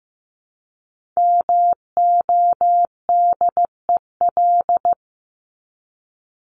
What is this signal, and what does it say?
Morse code, heard as a keyed tone, about 15 wpm, 700 Hz. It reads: MODEL
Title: Top 501-750 Words 15wpm